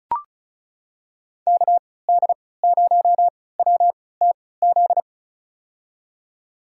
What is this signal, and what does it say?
Morse code, heard as a keyed tone, about 34 wpm, spaced as for 20 wpm, 700 Hz. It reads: KD0WTZ
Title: USA Call Signs - 3x Character Spacing - 35wpm